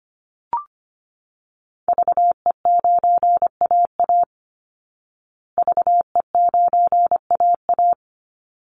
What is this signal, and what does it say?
Morse code, heard as a keyed tone, about 25 wpm, 700 Hz. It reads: VE9AA VE9AA
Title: Call Signs Repeated 3x - 25wpm